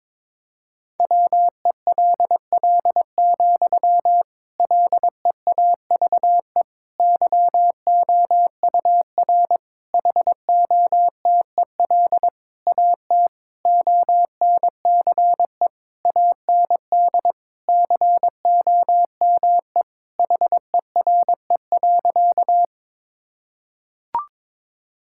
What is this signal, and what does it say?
Morse code, heard as a keyed tone, about 22 wpm, 700 Hz. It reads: WELL, LEAVE YOUR HOTEL AT ONCE AND COME HERE.
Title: Sentences from Top 1600 Words 22wpm